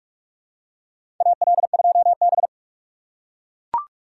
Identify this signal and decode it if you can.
Morse code, heard as a keyed tone, about 42 wpm, 700 Hz. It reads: AL2B
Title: USA Call Signs - 4 Characters - 2x Character Spacing - 45wpm